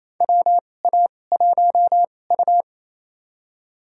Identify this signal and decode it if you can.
Morse code, heard as a keyed tone, about 28 wpm, spaced as for 20 wpm, 700 Hz. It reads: WA1U
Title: USA Call Signs - 4 Characters - 2x Character Spacing - 28wpm